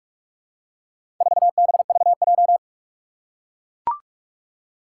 Morse code, heard as a keyed tone, about 44 wpm, 700 Hz. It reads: 4BVJ
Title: Four Challenging Letter-Number Combinations 45wpm